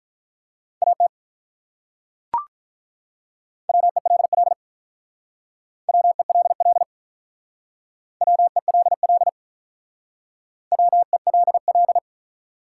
Morse code, heard as a keyed tone, about 45 wpm, 700 Hz. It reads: AT WELL WELL WELL WELL